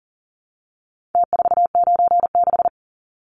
Morse code, heard as a keyed tone, about 40 wpm, 700 Hz. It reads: T496
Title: Numbers - Contest Style - 4-Digit 40wpm